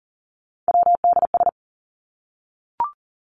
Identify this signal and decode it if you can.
Morse code, heard as a keyed tone, about 40 wpm, 700 Hz. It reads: WDS
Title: Abbreviations Repeated 3x - 40wpm